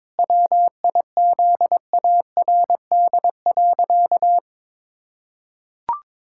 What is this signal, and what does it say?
Morse code, heard as a keyed tone, about 22 wpm, 700 Hz. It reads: WIZARD.
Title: Pangram Challenge v2 - 22wpm